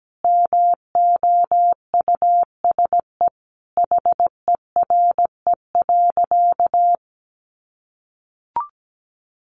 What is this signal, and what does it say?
Morse code, heard as a keyed tone, about 17 wpm, 700 Hz. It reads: MOUSE HERE.
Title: A Sound Of Thunder 17wpm